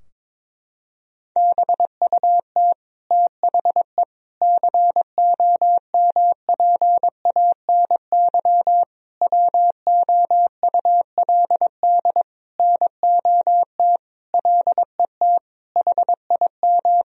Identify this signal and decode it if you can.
Morse code, heard as a keyed tone, about 22 wpm, 700 Hz. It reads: BUT THE COMPANY WOULD NOT LET HIM
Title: Sentences from Top 1000 Words - Review Part 3 - 22wpm